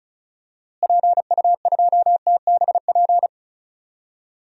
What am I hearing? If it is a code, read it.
Morse code, heard as a keyed tone, about 34 wpm, 700 Hz. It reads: PU2TBP
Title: DXCC-100 Call Signs - 6 Characters - Repeated 2x - 35wpm